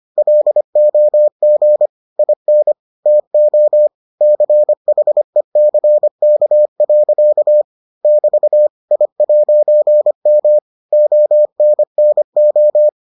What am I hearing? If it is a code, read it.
Morse code, heard as a keyed tone, about 25 wpm, 600 Hz. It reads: LOG IN TO CHECK. <BT> I'M ONNO